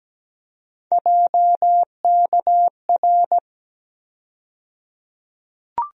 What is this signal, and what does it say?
Morse code, heard as a keyed tone, about 17 wpm, 700 Hz. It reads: JKR
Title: USA Call Sign Postfixes - 3 Letters 17wpm